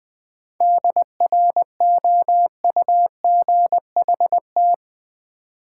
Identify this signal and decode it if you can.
Morse code, heard as a keyed tone, about 20 wpm, 700 Hz. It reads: DROUGHT
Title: 7 Letter Words - First 3 Letters Given - 20wpm